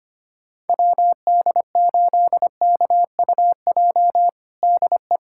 Morse code, heard as a keyed tone, about 25 wpm, 700 Hz. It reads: WD8KUJ DE
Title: USA Callsign DE Callsign 25wpm